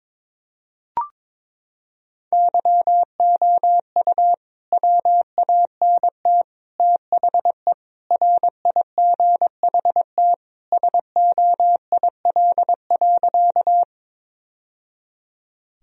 Morse code, heard as a keyed tone, about 22 wpm, 700 Hz. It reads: YOU WANT THE RIGHT SOIL.
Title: Sentences from Top 1000 Words - Review Part 4 - 22wpm